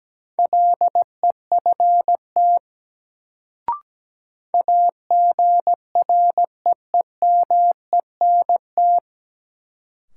Morse code, heard as a keyed tone, about 17 wpm, 700 Hz. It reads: LEFT AGREEMENT
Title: Sets of 5 Words Encore - New Word Drill - 17wpm